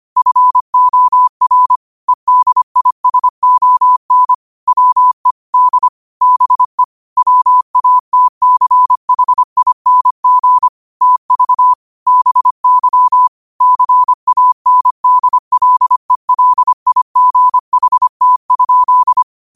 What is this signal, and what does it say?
Morse code, heard as a keyed tone, about 25 wpm, 1000 Hz. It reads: FOR EDISON WE D BE WATCHING TV BY CANDLELIGHT?
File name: qotdfiles / 25wpm